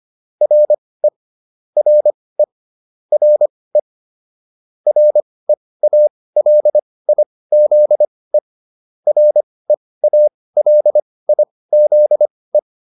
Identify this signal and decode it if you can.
Morse code, heard as a keyed tone, about 25 wpm, spaced as for 18 wpm, 600 Hz. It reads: RE RE RE REALIZE REALIZE